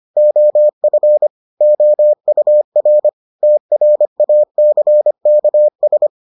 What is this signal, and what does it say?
Morse code, heard as a keyed tone, about 25 wpm, 600 Hz. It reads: OF OUR TRACKS